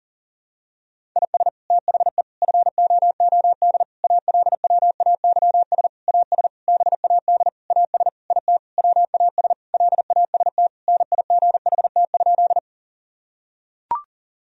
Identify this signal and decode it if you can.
Morse code, heard as a keyed tone, about 40 wpm, 700 Hz. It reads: IS THE FOOD ALWAYS AS BAD AS IT WAS LAST NIGHT?
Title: Sentences from Top 1000 Words - Review Part 1 - 40wpm